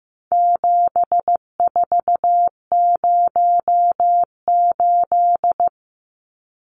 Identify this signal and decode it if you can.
Morse code, heard as a keyed tone, about 15 wpm, 700 Hz. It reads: 7408
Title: Numbers - 4-Digit 15wpm